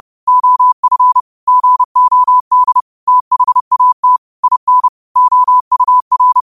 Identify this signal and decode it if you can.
Morse code, heard as a keyed tone, about 30 wpm, 1000 Hz. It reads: OF GOD THAT IN OUR
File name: qotdfiles / 30wpm